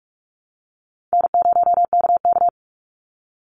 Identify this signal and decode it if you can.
Morse code, heard as a keyed tone, about 46 wpm, 700 Hz. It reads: N0KK
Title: Easy Call Signs - 45wpm